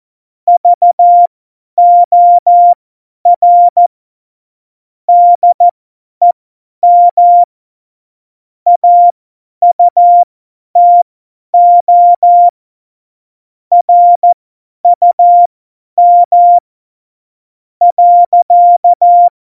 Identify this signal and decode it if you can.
Morse code, heard as a keyed tone, about 14 wpm, spaced as for 9 wpm, 700 Hz. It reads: VOR DEM AUTO RUM.
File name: Morsekurs / Witze / Witz 08. Januar 2019, 50 F70